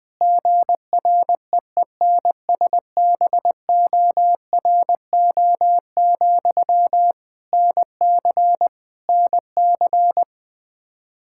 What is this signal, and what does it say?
Morse code, heard as a keyed tone, about 20 wpm, 700 Hz. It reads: GREENSBORO, NC NC
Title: K1USN SST QSO Practice 20wpm